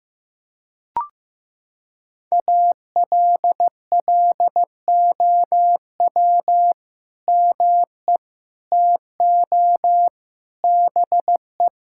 Morse code, heard as a keyed tone, about 15 wpm, 700 Hz. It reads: ALLOW ME TO BE
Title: Sentences from Top 1000 Words - Review Part 5 - 15wpm